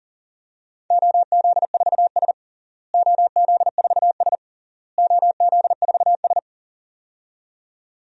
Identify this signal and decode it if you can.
Morse code, heard as a keyed tone, about 40 wpm, 700 Hz. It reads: OZ4S OZ4S OZ4S
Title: DXCC-100 Call Signs - 4 Characters - Repeated 3x - 40wpm